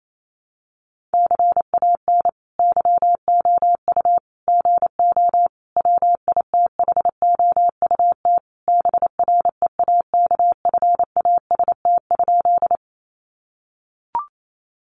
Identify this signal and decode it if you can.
Morse code, heard as a keyed tone, about 28 wpm, 700 Hz. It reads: CAN YOU GO WITHOUT BREAKFAST?
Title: Sentences from Top 1500 Words 28wpm